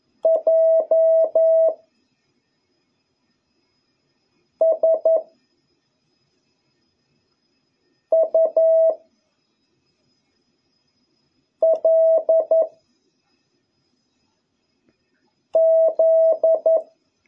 Morse code, heard as a keyed tone, about 11 wpm, 650 Hz. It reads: J S U L Z